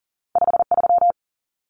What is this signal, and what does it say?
Morse code, heard as a keyed tone, about 40 wpm, 700 Hz. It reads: 53
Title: Common Cut Numbers - 2-Digit 40wpm